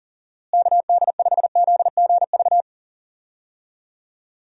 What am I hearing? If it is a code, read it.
Morse code, heard as a keyed tone, about 40 wpm, 700 Hz. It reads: KD5ZGV
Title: USA Call Signs - 6 Characters - Repeated 2x - 40wpm